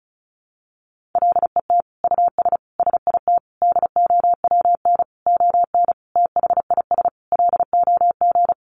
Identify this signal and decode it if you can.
Morse code, heard as a keyed tone, about 34 wpm, 700 Hz. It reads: LET US SIT DOWN ON THIS LOG
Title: Sentences from Top 1400 Words 35wpm